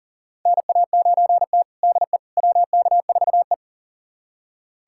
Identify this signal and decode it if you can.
Morse code, heard as a keyed tone, about 40 wpm, 700 Hz. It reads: NA9T DE WK4E
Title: USA Callsign DE Callsign - 4 Characters - 40wpm